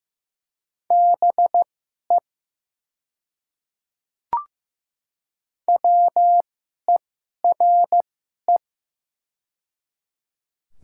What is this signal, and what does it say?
Morse code, heard as a keyed tone, about 15 wpm, spaced as for 11 wpm, 700 Hz. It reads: BE WERE
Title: 2x Character Spacing - Top 100 Words - 15wpm